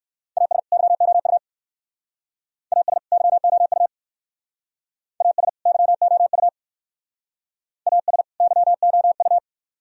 Morse code, heard as a keyed tone, about 59 wpm, 700 Hz. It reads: AS YOU AS YOU AS YOU AS YOU